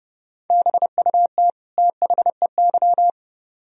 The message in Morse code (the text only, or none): BUT THEY